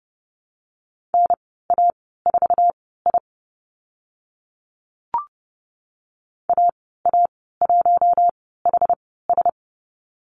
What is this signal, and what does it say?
Morse code, heard as a keyed tone, about 28 wpm, spaced as for 17 wpm, 700 Hz. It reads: NA4I AA1HS
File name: Lesson 15g - Number 5 - Call Signs 30wpm